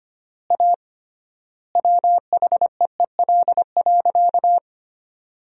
Morse code, heard as a keyed tone, about 25 wpm, 700 Hz. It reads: A WHEEL.
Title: 3x Word-Spacing - Sentences from Top 500 Words - 25wpm